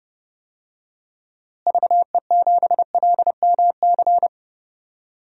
Morse code, heard as a keyed tone, about 30 wpm, 700 Hz. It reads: VE7LMC
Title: DXCC-100 Call Signs - 6 Characters - Repeated 3x - 30wpm